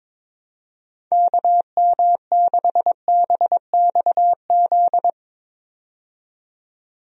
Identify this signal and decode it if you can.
Morse code, heard as a keyed tone, about 22 wpm, 700 Hz. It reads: KM6BXZ
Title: USA Call Signs - 6 Characters - 22wpm